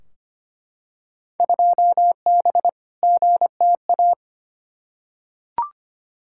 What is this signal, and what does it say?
Morse code, heard as a keyed tone, about 25 wpm, 700 Hz. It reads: 2B GTA
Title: Field Day Exchanges Encore - 25wpm